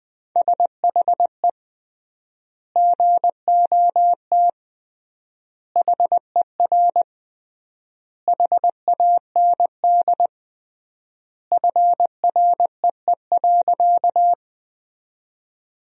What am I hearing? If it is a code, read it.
Morse code, heard as a keyed tone, about 20 wpm, 700 Hz. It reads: SHE GOT HER HAND FREE.
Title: 3x Word-Spacing - Sentences from Top 500 Words - 20wpm